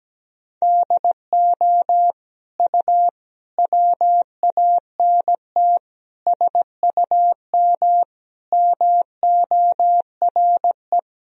Morse code, heard as a keyed tone, about 17 wpm, 700 Hz. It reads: DO U WANT SUM MORE
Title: Sentences from Top 300 Words and Abbreviations 17wpm